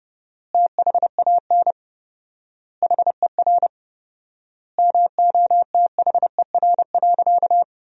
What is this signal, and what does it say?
Morse code, heard as a keyed tone, about 32 wpm, 700 Hz. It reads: THAN HER MOTHER.
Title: 4x Word-Spacing - Sentences from Top 400 Words - 30wpm